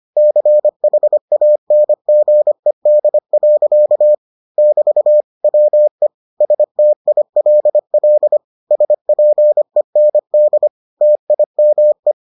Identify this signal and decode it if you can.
Morse code, heard as a keyed tone, about 25 wpm, 600 Hz. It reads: CHANGED. <BT> WE STILL SPEND TIME